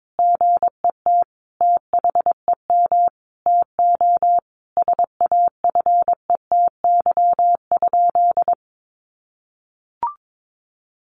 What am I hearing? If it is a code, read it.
Morse code, heard as a keyed tone, about 22 wpm, 700 Hz. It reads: GET THEM TO SAFETY?
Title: Sentences from Top 700 Words 22wpm